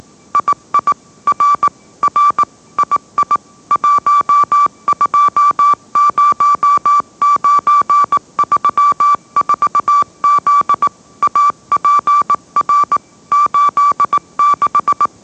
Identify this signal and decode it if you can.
Morse code, heard as a keyed tone, about 18 wpm, 1160 Hz. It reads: II R R II 120934Z APR 86
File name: mrx / mrx082 20wpm